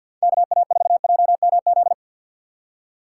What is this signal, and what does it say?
Morse code, heard as a keyed tone, about 51 wpm, 700 Hz. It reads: KA4JMZ
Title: USA Call Signs - 6 Characters - 6x Character Spacing - 50wpm